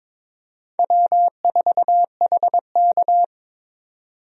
Morse code, heard as a keyed tone, about 22 wpm, 700 Hz. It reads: W4HK
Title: USA Call Signs - 4 Characters - 22wpm